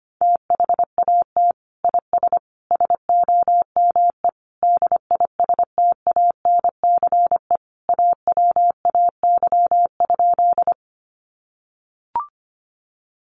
Morse code, heard as a keyed tone, about 25 wpm, 700 Hz. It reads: THAT IS SOME DISTANCE AWAY?
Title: Sentences from Top 1200 Words 25wpm